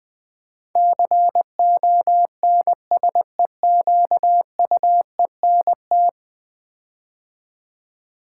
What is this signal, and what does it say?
Morse code, heard as a keyed tone, about 20 wpm, 700 Hz. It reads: CONSEQUENT
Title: Build A Word - 5 Letters At A Time - 12-Letter Words - 20wpm